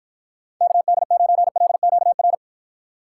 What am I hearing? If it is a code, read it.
Morse code, heard as a keyed tone, about 54 wpm, 700 Hz. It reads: KD9LQR